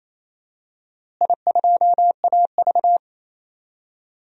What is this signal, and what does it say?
Morse code, heard as a keyed tone, about 28 wpm, 700 Hz. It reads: I2AV
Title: DXCC-100 Call Signs - 4 Characters - Repeated 2x - 28wpm